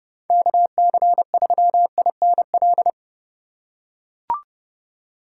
Morse code, heard as a keyed tone, about 30 wpm, 700 Hz. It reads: KC3INL
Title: USA Call Signs - 2x Character Spacing - 30wpm